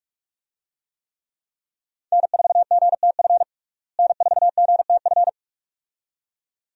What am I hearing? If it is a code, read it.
Morse code, heard as a keyed tone, about 42 wpm, 700 Hz. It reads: N4GTF N4GTF
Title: USA Call Signs - 5 Characters - Repeated 2x - 45wpm